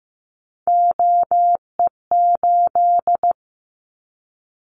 Morse code, heard as a keyed tone, about 15 wpm, 700 Hz. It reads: OE8
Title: Callsign Prefixes for Top 100 DXCC - 15wpm